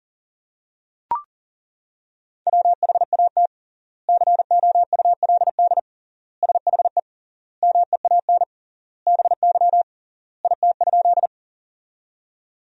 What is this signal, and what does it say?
Morse code, heard as a keyed tone, about 40 wpm, 700 Hz. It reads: WHAT COULD SHE MEAN BY IT?
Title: 3x Word-Spacing - Sentences from Top 200 Words - 40wpm